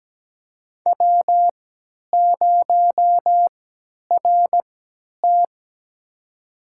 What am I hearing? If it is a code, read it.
Morse code, heard as a keyed tone, about 17 wpm, spaced as for 10 wpm, 700 Hz. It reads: W0RT
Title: USA Call Signs - 4 Characters - 3x Character Spacing - 17wpm